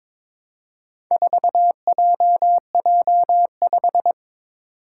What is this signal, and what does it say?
Morse code, heard as a keyed tone, about 22 wpm, 700 Hz. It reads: 4JJ5
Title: Four Challenging Letter-Number Combinations 22wpm